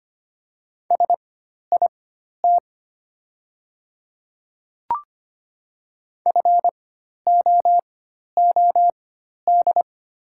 Morse code, heard as a keyed tone, about 25 wpm, spaced as for 12 wpm, 700 Hz. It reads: SIT FOOD
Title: 4x Character Spacing - Top 500 Words - 25wpm